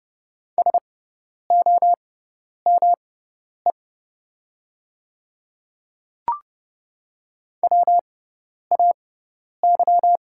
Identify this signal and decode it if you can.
Morse code, heard as a keyed tone, about 29 wpm, spaced as for 10 wpm, 700 Hz. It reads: SOME WAY
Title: 6x Character Spacing - Top 100 Words - 30wpm